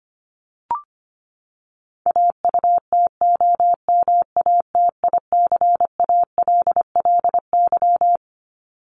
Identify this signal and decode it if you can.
Morse code, heard as a keyed tone, about 25 wpm, 700 Hz. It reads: AUTOMATICALLY
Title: Top 1751-2000 Words 25wpm